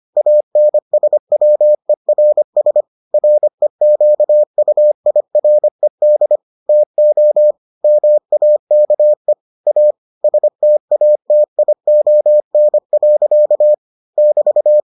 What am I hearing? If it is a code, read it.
Morse code, heard as a keyed tone, about 25 wpm, 600 Hz. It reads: ANSWERS REQUIRED TO MAKE A STATION. <BT>